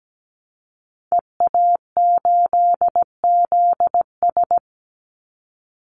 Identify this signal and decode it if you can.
Morse code, heard as a keyed tone, about 17 wpm, 700 Hz. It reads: EA8ZS
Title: Call Signs - 17wpm